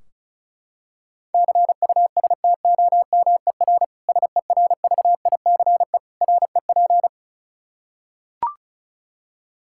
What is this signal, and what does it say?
Morse code, heard as a keyed tone, about 36 wpm, 700 Hz. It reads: CUSTOMER SERVICE REP
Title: Practice QSO - Exchange 3 - More Info - 35wpm